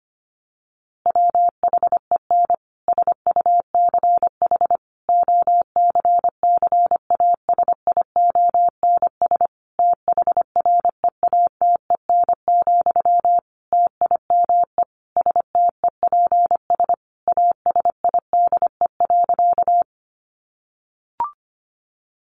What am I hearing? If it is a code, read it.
Morse code, heard as a keyed tone, about 25 wpm, 700 Hz. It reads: WHEN SUCH OCCASIONS THREATEN, TIME STEPS ASIDE.
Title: A Sound Of Thunder 25wpm